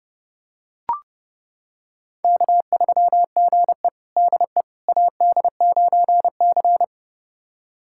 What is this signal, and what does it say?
Morse code, heard as a keyed tone, about 30 wpm, 700 Hz. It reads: K3GE DE AD9C
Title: USA Callsign DE Callsign - 4 Characters - 30wpm